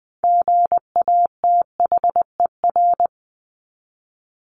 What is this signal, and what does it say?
Morse code, heard as a keyed tone, about 20 wpm, 700 Hz. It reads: GATHER
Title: Top 1501-1750 Words 20wpm